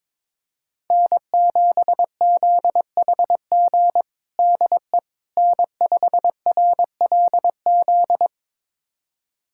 Morse code, heard as a keyed tone, about 22 wpm, 700 Hz. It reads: N7ZHG DE N5RLZ
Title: USA Callsign DE Callsign - 5 Characters - 22wpm